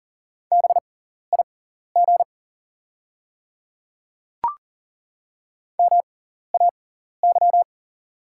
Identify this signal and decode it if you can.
Morse code, heard as a keyed tone, about 40 wpm, spaced as for 14 wpm, 700 Hz. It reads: BIG MAY
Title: 6x Character Spacing - Top 200 Words - 40wpm